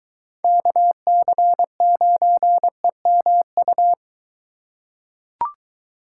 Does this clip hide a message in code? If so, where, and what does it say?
Yes: Morse code from 0.4 to 3.9 s, heard as a keyed tone, about 23 wpm, 700 Hz. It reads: KC9EMU